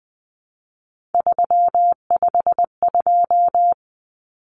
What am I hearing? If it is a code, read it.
Morse code, heard as a keyed tone, about 20 wpm, 700 Hz. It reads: 352
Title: Numbers - 3-Digit 20wpm